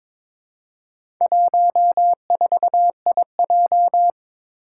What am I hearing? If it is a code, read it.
Morse code, heard as a keyed tone, about 22 wpm, 700 Hz. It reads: 14IJ
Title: Four Challenging Letter-Number Combinations 22wpm